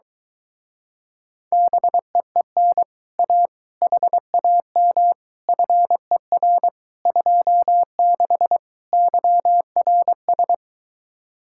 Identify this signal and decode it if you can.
Morse code, heard as a keyed tone, about 22 wpm, 700 Hz. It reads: BEEN A HAM FER 26 YRS